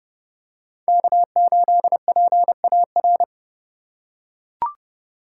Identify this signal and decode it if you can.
Morse code, heard as a keyed tone, about 30 wpm, 700 Hz. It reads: K8PAR
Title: USA Call Signs - 5 Characters - 4x Character Spacing - 30wpm